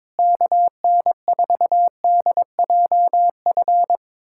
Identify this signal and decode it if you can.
Morse code, heard as a keyed tone, about 22 wpm, 700 Hz. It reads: KN4DJF
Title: USA Callsign DE Callsign - 6 Characters - 22wpm